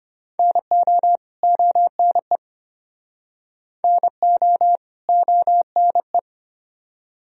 Morse code, heard as a keyed tone, about 33 wpm, 700 Hz. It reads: NO ONE NO ONE